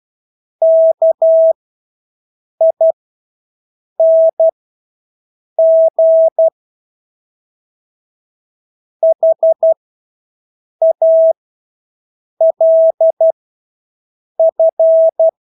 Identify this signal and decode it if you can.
Morse code, heard as a keyed tone, about 12 wpm, spaced as for 6 wpm, 650 Hz. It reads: KING HALF
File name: Engelske ord 12-6 wpm